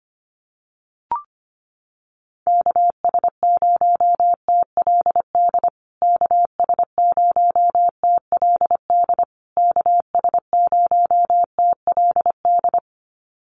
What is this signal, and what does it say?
Morse code, heard as a keyed tone, about 25 wpm, 700 Hz. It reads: KS0TLD KS0TLD KS0TLD
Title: USA Call Signs Repeated 3x 25wpm